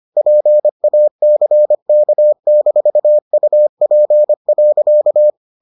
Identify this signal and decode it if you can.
Morse code, heard as a keyed tone, about 25 wpm, 600 Hz. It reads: PACK-UP.